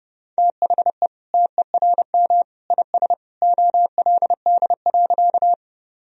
Morse code, heard as a keyed tone, about 30 wpm, 700 Hz. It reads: THE TERM IS OLD.
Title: Sentences from Top 1000 Words - Review Part 4 - 30wpm